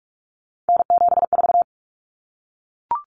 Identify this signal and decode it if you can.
Morse code, heard as a keyed tone, about 44 wpm, 700 Hz. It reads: N74
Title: Common Cut Numbers - 3-Digit 45wpm